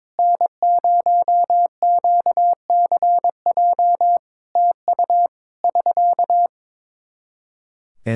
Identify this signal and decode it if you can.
Morse code, heard as a keyed tone, about 22 wpm, 700 Hz. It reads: N0QCJ TU <SK>
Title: Practice QSO - Exchange 4 - Ending - 22wpm